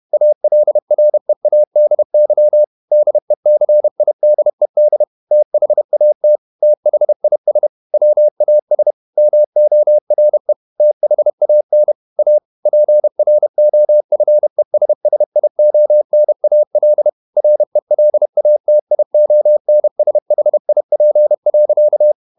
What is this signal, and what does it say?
Morse code, heard as a keyed tone, about 31 wpm, 600 Hz. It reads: ALREADY DECIDED THAT THIS WAS MORE THAN A PROFESSIONAL RELATIONSHIP.
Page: Never 31wpm | CW med Gnister